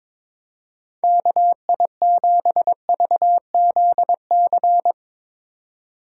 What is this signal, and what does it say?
Morse code, heard as a keyed tone, about 21 wpm, 700 Hz. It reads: KI7VZC
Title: USA Call Signs - 6x Character Spacing - 22wpm